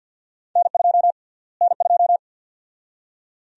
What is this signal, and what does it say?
Morse code, heard as a keyed tone, about 50 wpm, 700 Hz. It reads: N2 N2
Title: USA Call Sign Prefixes - Repeated 2x - 50wpm